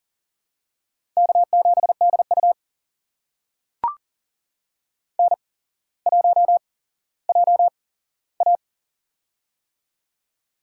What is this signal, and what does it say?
Morse code, heard as a keyed tone, about 40 wpm, spaced as for 17 wpm, 700 Hz. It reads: K7DU N1JA
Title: USA Call Signs - 4 Characters - 8x Character Spacing - 40wpm